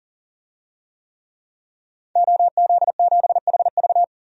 Morse code, heard as a keyed tone, about 40 wpm, 700 Hz. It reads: OZ7HV
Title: DXCC-100 Call Signs - 5 Characters - 40wpm